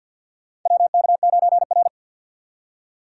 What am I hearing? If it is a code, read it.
Morse code, heard as a keyed tone, about 53 wpm, 690 Hz. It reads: WK9R
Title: USA Call Signs - 6x Character Spacing - 50wpm